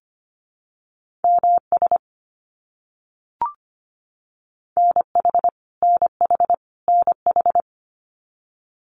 Morse code, heard as a keyed tone, about 25 wpm, 700 Hz. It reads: MS NH NH NH
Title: US States Repeated 3x 25wpm